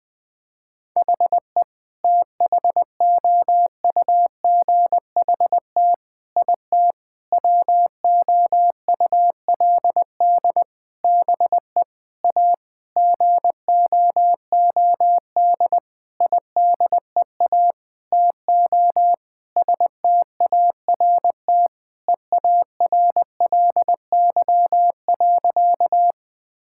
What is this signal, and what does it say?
Morse code, heard as a keyed tone, about 20 wpm, 700 Hz. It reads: HE THOUGHT IT WOULD BE A GOOD IDEA TO START EARLY.
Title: Sentences with 8-Word Phrases - 20wpm